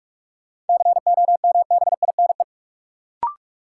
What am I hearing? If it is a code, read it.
Morse code, heard as a keyed tone, about 47 wpm, 700 Hz. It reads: COMBINE
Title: Sentences from Top 1600 Words - New Word Drill - 45wpm